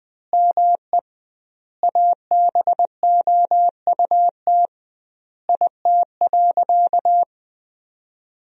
Morse code, heard as a keyed tone, about 21 wpm, 700 Hz. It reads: ME ABOUT IT.
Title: 2x Word-Spacing - Sentences from Top 200 Words - 20wpm